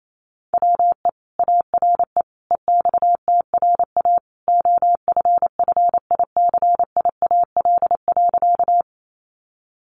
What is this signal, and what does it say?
Morse code, heard as a keyed tone, about 28 wpm, 700 Hz. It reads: WE ARE EXTRA OFFICIAL.
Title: Sentences from Top 1000 Words 28wpm